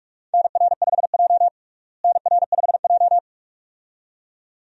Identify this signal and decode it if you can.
Morse code, heard as a keyed tone, about 41 wpm, 700 Hz. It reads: NR5J NR5J
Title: USA Call Signs - 4 Characters - Repeated 2x - 45wpm